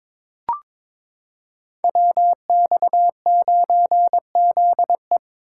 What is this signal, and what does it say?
Morse code, heard as a keyed tone, about 21 wpm, 700 Hz. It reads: WX9ZE
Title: Code Groups - 5 Letter:Numbers - 22wpm